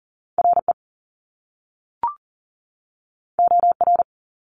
Morse code, heard as a keyed tone, about 40 wpm, 700 Hz. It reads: RE OR
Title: Top Two Letter Combinations in Words 40wpm